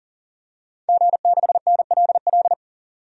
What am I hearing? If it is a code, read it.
Morse code, heard as a keyed tone, about 37 wpm, 700 Hz. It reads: G6NLL
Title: DXCC-100 Call Signs - 5 Characters - 40wpm